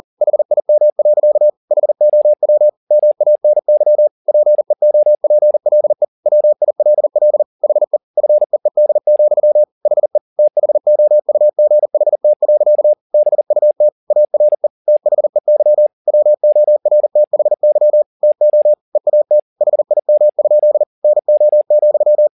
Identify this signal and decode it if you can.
Morse code, heard as a keyed tone, about 40 wpm, 600 Hz. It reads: HIM. HOW MANY PEOPLE WILL HE FEED, HE THOUGHT. BUT ARE THEY WORTHY TO EAT HIM? NO,
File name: the old man and the sea 2 40wpm